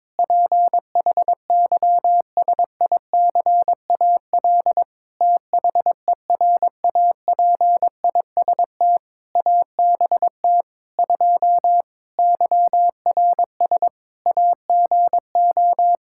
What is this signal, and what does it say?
Morse code, heard as a keyed tone, about 22 wpm, 700 Hz. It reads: PHYSICAL THERAPIST ABT 2 YRS AGO
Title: K1USN SST QSO Practice 22wpm